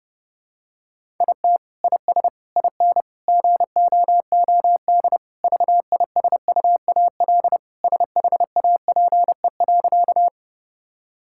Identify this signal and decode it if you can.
Morse code, heard as a keyed tone, about 30 wpm, 700 Hz. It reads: IT IS IN GOOD VISUAL SHAPE.
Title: Sentences from Top 1000 Words - Review Part 6 - 30wpm